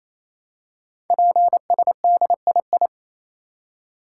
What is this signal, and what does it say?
Morse code, heard as a keyed tone, about 29 wpm, 700 Hz. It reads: PSDII
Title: Code Groups - 5 Letter:Numbers - 28wpm